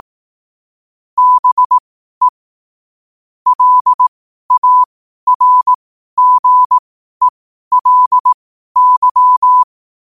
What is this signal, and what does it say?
Morse code, heard as a keyed tone, about 18 wpm, spaced as for 12 wpm, 1000 Hz. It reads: BE LARGELY
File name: qotdfiles / 12wpm